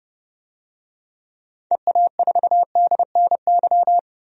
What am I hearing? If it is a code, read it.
Morse code, heard as a keyed tone, about 30 wpm, 700 Hz. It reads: EA4DNY
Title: DXCC-100 Call Signs - 6 Characters - 30wpm